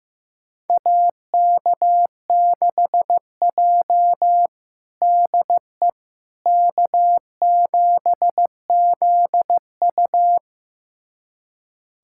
Morse code, heard as a keyed tone, about 15 wpm, 700 Hz. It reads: AK6J DE K7ZU
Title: USA Callsign DE Callsign - 4 Characters - 15wpm